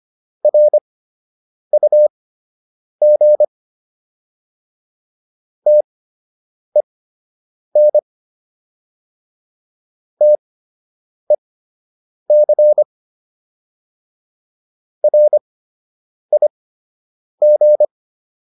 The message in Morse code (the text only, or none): RUG TEN TEC RIG